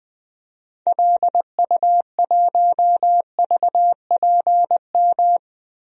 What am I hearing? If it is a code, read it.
Morse code, heard as a keyed tone, about 20 wpm, 700 Hz. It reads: LU1VPM
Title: DXCC-100 Call Signs - 6 Characters - Repeated 2x - 20wpm